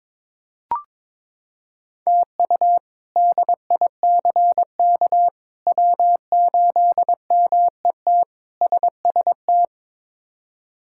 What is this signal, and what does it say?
Morse code, heard as a keyed tone, about 22 wpm, 700 Hz. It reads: TU DICK W8MET SST
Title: K1USN SST QSO Practice - 22wpm